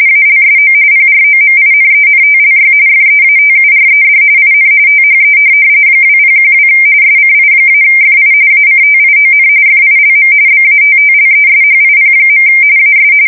RTTY, about 45.45 baud. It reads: ALSO PROMINENTLY FEATURED IN THE 2012 SCI-FI ACTION FILM BATTLESHIP. AS
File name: fsk170 / USS-Missouri